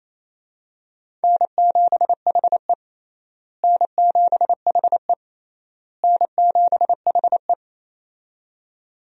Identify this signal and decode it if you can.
Morse code, heard as a keyed tone, about 28 wpm, 700 Hz. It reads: N7HE N7HE N7HE
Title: USA Call Signs - 4 Characters - Repeated 3x - 28wpm